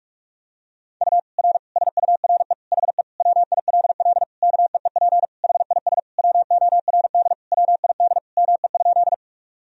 Morse code, heard as a keyed tone, about 41 wpm, 700 Hz. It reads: U R SURE HE WILL KEEP HIS WORD WID ME?
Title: Sentences from Top 300 Words and Abbreviations 45wpm